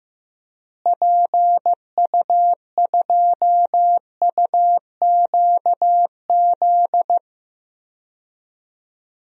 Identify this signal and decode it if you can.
Morse code, heard as a keyed tone, about 15 wpm, 700 Hz. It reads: PU2UQZ
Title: DXCC-100 Call Signs - 6 Characters - Repeated 3x - 15wpm